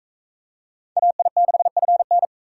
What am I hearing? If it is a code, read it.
Morse code, heard as a keyed tone, about 38 wpm, 700 Hz. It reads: AI6FN